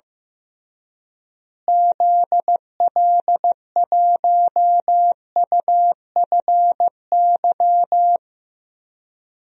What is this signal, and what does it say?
Morse code, heard as a keyed tone, about 15 wpm, 700 Hz. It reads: ZL1UFY
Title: DXCC-100 Call Signs - 6 Characters - Repeated 2x - 15wpm